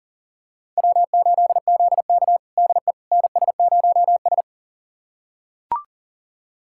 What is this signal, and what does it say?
Morse code, heard as a keyed tone, about 40 wpm, 700 Hz. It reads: W8ZK DE NS0S
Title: USA Callsign DE Callsign - 4 Characters - 40wpm